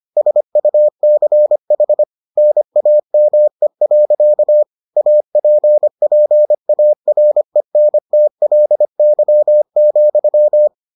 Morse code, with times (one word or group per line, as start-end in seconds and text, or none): 0.2-2.0 SUCH
2.4-4.6 NAME.
5.0-10.7 APPARENTLY,